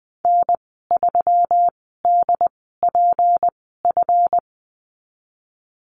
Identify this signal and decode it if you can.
Morse code, heard as a keyed tone, about 19 wpm, spaced as for 14 wpm, 700 Hz. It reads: N3DPF
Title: 154 Lesson 25g - Letter p - Call Signs 20wpm